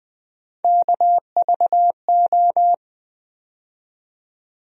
Morse code, heard as a keyed tone, about 20 wpm, 700 Hz. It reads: KVO
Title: USA Call Sign Postfixes - 3 Letters 20wpm